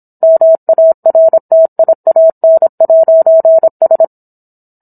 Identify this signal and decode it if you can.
Morse code, heard as a keyed tone, about 27 wpm, 650 Hz. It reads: MARTIAN'S